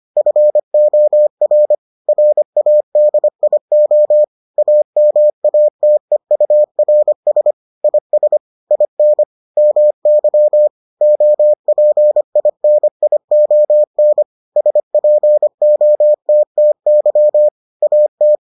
Morse code, heard as a keyed tone, about 25 wpm, 600 Hz. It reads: FOR RADIO AMATEURS IS IN MY OPINION SPOTTY AT